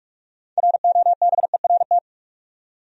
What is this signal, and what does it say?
Morse code, heard as a keyed tone, about 44 wpm, 700 Hz. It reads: ROBERT
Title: USA Common Names - 45wpm